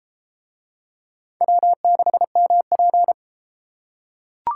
Morse code, heard as a keyed tone, about 29 wpm, 700 Hz. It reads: W6MP